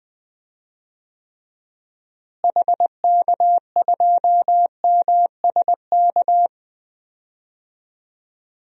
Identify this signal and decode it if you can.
Morse code, heard as a keyed tone, about 20 wpm, 700 Hz. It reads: HK2MSK
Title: DXCC-100 Call Signs - 6 Characters - Repeated 3x - 20wpm